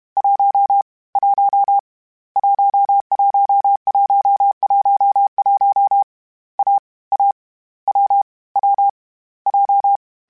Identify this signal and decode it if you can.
Morse code, heard as a keyed tone, about 31 wpm, 800 Hz. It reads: 1 1 11111 A A W W J
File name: Difficult Combos / Difficult Combos-mp3s / AWJ1 @30